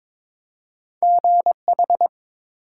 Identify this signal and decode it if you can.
Morse code, heard as a keyed tone, about 22 wpm, 700 Hz. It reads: GH
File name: ARRL Sections v2 - 22wpm